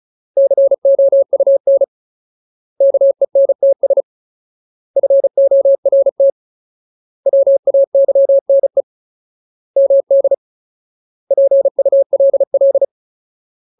Morse code, heard as a keyed tone, about 34 wpm, 550 Hz. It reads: COUN KENTS FORT WAYNE MD PULL
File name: Ultimate Word List / 35WPM Ultimate